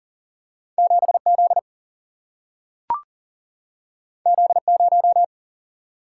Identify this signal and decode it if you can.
Morse code, heard as a keyed tone, about 40 wpm, 700 Hz. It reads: 7Z Z0
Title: Single-Letter Single-Number 40wpm